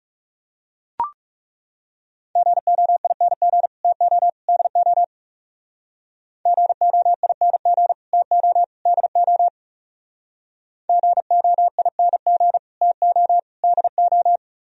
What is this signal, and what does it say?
Morse code, heard as a keyed tone, about 38 wpm, 700 Hz. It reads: GOING TO DO GOING TO DO GOING TO DO